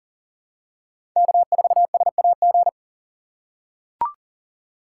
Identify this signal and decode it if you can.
Morse code, heard as a keyed tone, about 40 wpm, 700 Hz. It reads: K4SAG
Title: USA Call Signs - 5 Characters - 2x Character Spacing - 40wpm